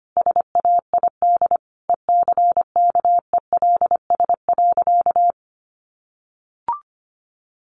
Morse code, heard as a keyed tone, about 25 wpm, 700 Hz. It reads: SAID ECKELS.
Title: A Sound Of Thunder 25wpm